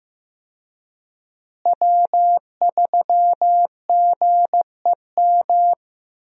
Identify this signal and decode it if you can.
Morse code, heard as a keyed tone, about 15 wpm, 700 Hz. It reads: W3GEM
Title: USA Call Signs - 5 Characters - Repeated 2x - 15wpm